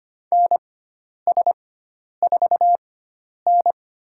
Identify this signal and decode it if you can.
Morse code, heard as a keyed tone, about 25 wpm, spaced as for 10 wpm, 700 Hz. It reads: NS4N
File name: Lesson 16g - Letter c - Call Signs 25wpm